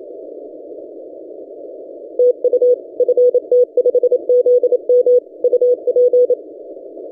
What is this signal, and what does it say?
Morse code, heard as a keyed tone, about 28 wpm, 490 Hz. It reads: TU FT5ZM UP
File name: ft5zm-17cw-2014-01-27-0120z-lp